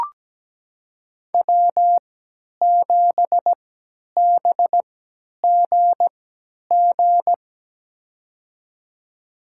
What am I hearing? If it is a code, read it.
Morse code, heard as a keyed tone, about 17 wpm, spaced as for 10 wpm, 700 Hz. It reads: W7BGG
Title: USA Call Signs - 5 Characters - 3x Character Spacing - 17wpm